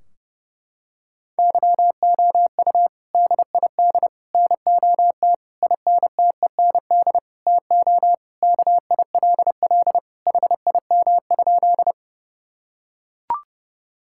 Morse code, heard as a keyed tone, about 30 wpm, 700 Hz. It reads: YOU DID NOT INTEND TO KILL HIM?
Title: Sentences from Top 1600 Words 30wpm